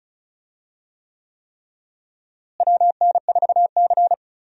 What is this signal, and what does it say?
Morse code, heard as a keyed tone, about 36 wpm, 700 Hz. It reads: WN4C
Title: USA Call Signs - 4 Characters - Repeated 3x - 35wpm